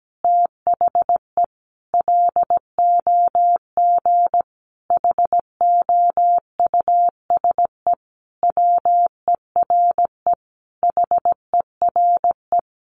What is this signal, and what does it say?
Morse code, heard as a keyed tone, about 17 wpm, 700 Hz. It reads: THE LOG HOUSE WERE HERE
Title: Sentences from Top 1400 Words 17wpm